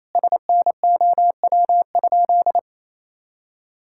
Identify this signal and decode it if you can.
Morse code, heard as a keyed tone, about 28 wpm, 700 Hz. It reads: SNOW?
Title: Sentences from Top 1600 Words 28wpm